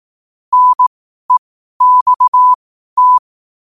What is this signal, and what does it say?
Morse code, heard as a keyed tone, about 18 wpm, spaced as for 12 wpm, 1000 Hz. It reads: NEXT